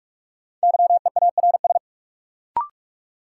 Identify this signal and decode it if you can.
Morse code, heard as a keyed tone, about 47 wpm, 700 Hz. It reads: YEARS
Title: Sentences from Top 200 Words - New Word Drill - 45wpm